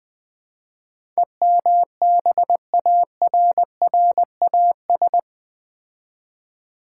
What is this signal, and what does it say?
Morse code, heard as a keyed tone, about 20 wpm, 700 Hz. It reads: EMBARRAS
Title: Build A Word - 4 Letters At A Time - 12-Letter Words - 20wpm